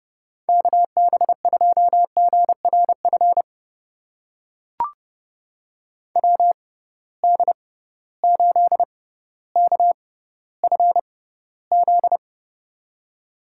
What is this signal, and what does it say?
Morse code, heard as a keyed tone, about 30 wpm, spaced as for 15 wpm, 700 Hz. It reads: KB2GRF WD8KFZ
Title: USA Call Signs - 6 Characters - 6x Character Spacing - 30wpm